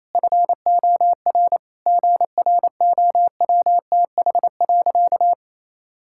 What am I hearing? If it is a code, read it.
Morse code, heard as a keyed tone, about 28 wpm, 700 Hz. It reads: FOR GROWTH.
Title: Sentences from Top 1000 Words - Review Part 5 - 28wpm